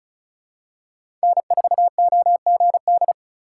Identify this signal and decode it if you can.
Morse code, heard as a keyed tone, about 34 wpm, 700 Hz. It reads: N4OGD
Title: USA Call Signs - 5 Characters - 3x Character Spacing - 35wpm